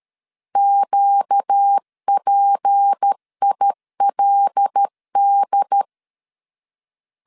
Morse code, heard as a keyed tone, about 13 wpm, 800 Hz. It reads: QPILD
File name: LEZIONI / 6 PKX0 / 6 P 12wpm 2033